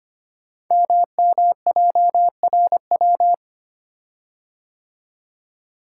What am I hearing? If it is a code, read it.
Morse code, heard as a keyed tone, about 25 wpm, 700 Hz. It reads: MMJRW
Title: Code Groups - 5 Letters - 25wpm